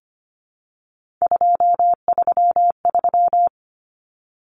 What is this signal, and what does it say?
Morse code, heard as a keyed tone, about 25 wpm, 700 Hz. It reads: 233
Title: Numbers - 3-Digit 25wpm